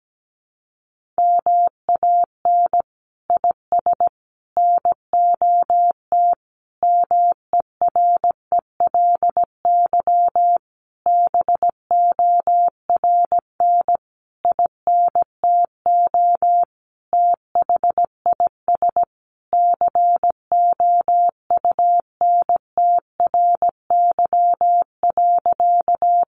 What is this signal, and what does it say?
Morse code, heard as a keyed tone, about 17 wpm, 700 Hz. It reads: MAN IS NOT MERELY BORN INTO THIS COUNTRY.
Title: Sentences from Top 1400 Words 17wpm